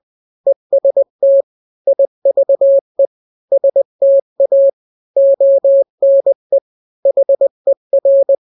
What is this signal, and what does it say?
Morse code, heard as a keyed tone, about 20 wpm, 550 Hz. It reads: EST IVE STA ONE HER
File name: n-grams / 3-grams 20 18 WPM